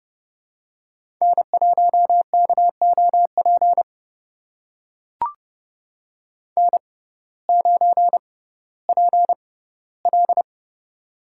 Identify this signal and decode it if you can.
Morse code, heard as a keyed tone, about 30 wpm, spaced as for 10 wpm, 700 Hz. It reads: N1KO PN9PL
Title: USA Call Signs - 6x Character Spacing - 30wpm